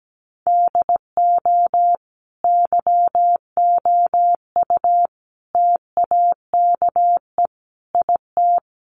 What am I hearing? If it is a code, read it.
Morse code, heard as a keyed tone, about 17 wpm, 700 Hz. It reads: DO YOU TAKE IT
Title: Sentences from Top 1200 Words 17wpm